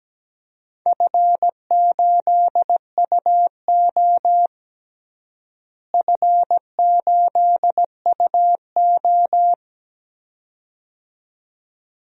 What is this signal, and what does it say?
Morse code, heard as a keyed tone, about 17 wpm, 700 Hz. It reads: F8UO F8UO
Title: DXCC-100 Call Signs - 4 Characters - Repeated 2x - 17wpm